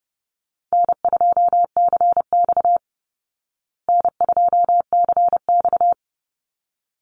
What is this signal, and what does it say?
Morse code, heard as a keyed tone, about 28 wpm, 700 Hz. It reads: N2CX N2CX
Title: Call Signs Repeated 3x - 30wpm